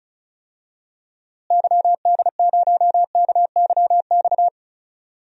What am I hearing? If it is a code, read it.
Morse code, heard as a keyed tone, about 36 wpm, 700 Hz. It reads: YD0KYX
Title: DXCC-100 Call Signs - 6 Characters - Repeated 2x - 35wpm